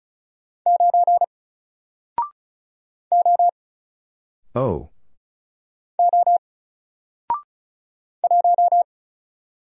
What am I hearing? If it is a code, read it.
Morse code, heard as a keyed tone, about 36 wpm, 700 Hz. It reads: 9 O O 1
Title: Single Letter-Number Alt 35wpm